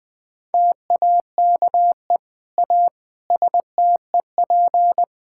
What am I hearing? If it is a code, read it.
Morse code, heard as a keyed tone, about 20 wpm, 700 Hz. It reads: TAKE A STEP
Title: Sentences with 8-Word Phrases - 20wpm